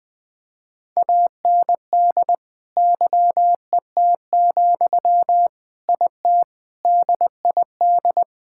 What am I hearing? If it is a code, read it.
Morse code, heard as a keyed tone, about 21 wpm, 700 Hz. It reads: AND YET, IT DID
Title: Sentences from Top 1600 Words 20wpm